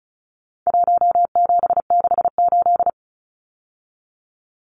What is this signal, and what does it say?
Morse code, heard as a keyed tone, about 34 wpm, 700 Hz. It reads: 1768
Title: Numbers - Contest Style - 4-Digit 35wpm